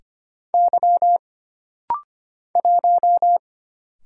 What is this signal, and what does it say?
Morse code, heard as a keyed tone, about 25 wpm, 700 Hz. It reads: Y 1
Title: Single Letter-Number Alt - Rapid-Fire 25wpm